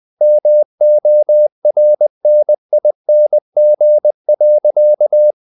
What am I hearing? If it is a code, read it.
Morse code, heard as a keyed tone, about 20 wpm, 600 Hz. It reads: MORNING.
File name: the old man and the sea / the old man and the sea part 1 20wpm